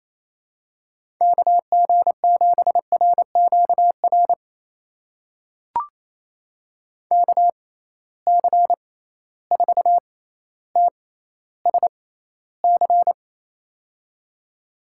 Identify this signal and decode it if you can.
Morse code, heard as a keyed tone, about 28 wpm, spaced as for 14 wpm, 700 Hz. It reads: KG7RQR KC4TSC
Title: USA Call Signs - 6 Characters - 6x Character Spacing - 28wpm